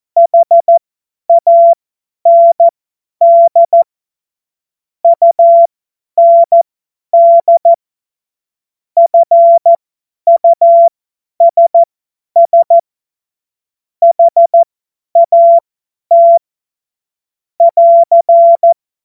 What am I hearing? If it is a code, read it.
Morse code, heard as a keyed tone, about 14 wpm, spaced as for 10 wpm, 680 Hz. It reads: HAND UND FUSS HAT <AR>
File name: Witz-05-August-2025-50-F70